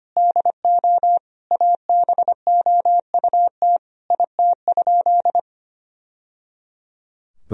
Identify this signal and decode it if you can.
Morse code, heard as a keyed tone, about 25 wpm, 700 Hz. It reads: DO ABOUT IT?
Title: Sentences from Top 1000 Words - Review Part 4 - 25wpm